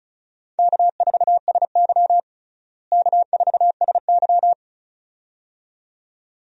Morse code, heard as a keyed tone, about 36 wpm, 700 Hz. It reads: K4SY K4SY
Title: USA Call Signs - 4 Characters - Repeated 2x - 35wpm